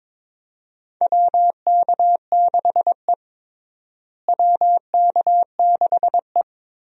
Morse code, heard as a keyed tone, about 22 wpm, 700 Hz. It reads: WK6E WK6E
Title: USA Call Signs - 4 Characters - Repeated 3x - 22wpm